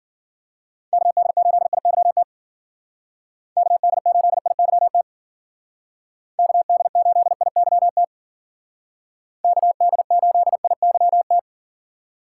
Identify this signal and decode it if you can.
Morse code, heard as a keyed tone, about 52 wpm, 700 Hz. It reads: KD8IYT KD8IYT KD8IYT KD8IYT